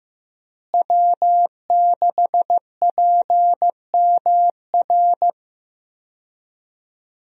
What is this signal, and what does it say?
Morse code, heard as a keyed tone, about 15 wpm, 700 Hz. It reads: W6PMR
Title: USA Call Signs - 5 Characters - Repeated 3x - 15wpm